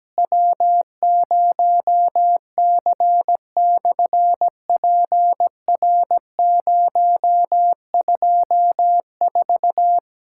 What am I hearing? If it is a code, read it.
Morse code, heard as a keyed tone, about 17 wpm, 700 Hz. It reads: W0C/PR024
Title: SOTA QSO Practice 17wpm